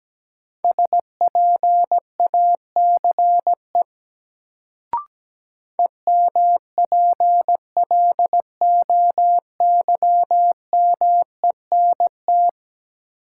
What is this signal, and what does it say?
Morse code, heard as a keyed tone, about 17 wpm, 700 Hz. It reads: SPACE EMPLOYMENT
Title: Sentences from Top 1500 Words - New Word Drill Alt - 17wpm